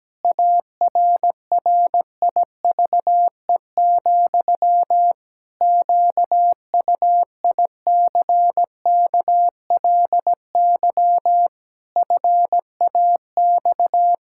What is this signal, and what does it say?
Morse code, heard as a keyed tone, about 17 wpm, 700 Hz. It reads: ARRIVE, QUICKLY FAX
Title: Pangram Challenge v2 - 17wpm